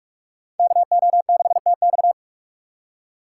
Morse code, heard as a keyed tone, about 46 wpm, 700 Hz. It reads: KO6TX